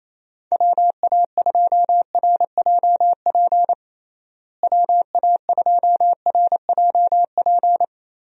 Morse code, heard as a keyed tone, about 27 wpm, 700 Hz. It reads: WA2RJP WA2RJP
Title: USA Call Signs - 6 Characters - Repeated 2x - 28wpm